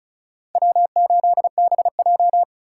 Morse code, heard as a keyed tone, about 34 wpm, 700 Hz. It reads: W8BJ
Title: USA Call Signs - 4 Characters - 4x Character Spacing - 35wpm